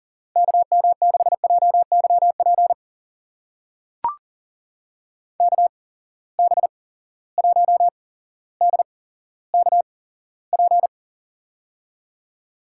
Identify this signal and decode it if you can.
Morse code, heard as a keyed tone, about 40 wpm, spaced as for 17 wpm, 700 Hz. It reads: KM6JYP KB1DKP
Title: USA Call Signs - 6 Characters - 8x Character Spacing - 40wpm